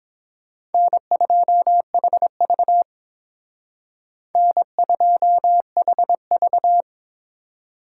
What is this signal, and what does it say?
Morse code, heard as a keyed tone, about 29 wpm, 700 Hz. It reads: N2HV N2HV